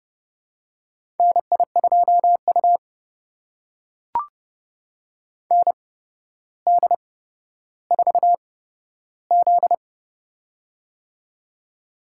Morse code, heard as a keyed tone, about 29 wpm, spaced as for 13 wpm, 700 Hz. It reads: NI2U ND4Z
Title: USA Call Signs - 4 Characters - 8x Character Spacing - 30wpm